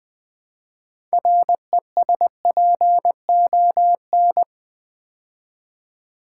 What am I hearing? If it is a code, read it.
Morse code, heard as a keyed tone, about 20 wpm, 700 Hz. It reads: RESPON
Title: Build A Word - 3 Letters At A Time - 8-Letter Words - 20wpm